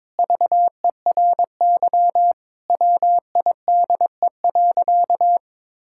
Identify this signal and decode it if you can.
Morse code, heard as a keyed tone, about 23 wpm, 700 Hz. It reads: VERY WIDE.
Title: Sentences from Top 1000 Words - Review Part 6 - 22wpm